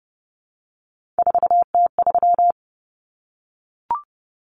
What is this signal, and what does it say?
Morse code, heard as a keyed tone, about 30 wpm, 700 Hz. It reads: 4T3
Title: Common Cut Numbers - 3-Digit 30wpm